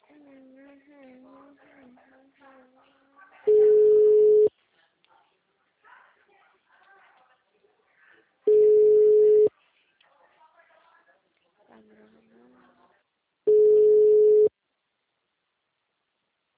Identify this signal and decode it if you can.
Ringback tone (Europe, 425 Hz), with speech.